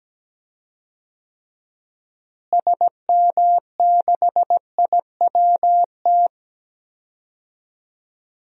Morse code, heard as a keyed tone, about 17 wpm, 700 Hz. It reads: SM6IWT
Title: DXCC-100 Call Signs - 6 Characters - Repeated 2x - 17wpm